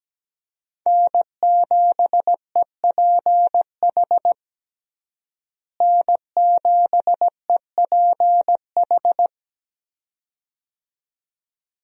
Morse code, heard as a keyed tone, about 17 wpm, 700 Hz. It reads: N7EPH N7EPH
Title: USA Call Signs - 5 Characters - Repeated 2x - 17wpm